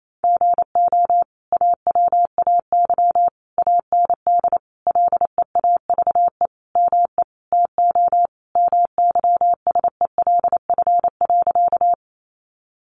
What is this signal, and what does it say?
Morse code, heard as a keyed tone, about 28 wpm, 700 Hz. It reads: GO AWAY AND LEAVE ME TO MYSELF.
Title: Sentences from Top 1300 Words 28wpm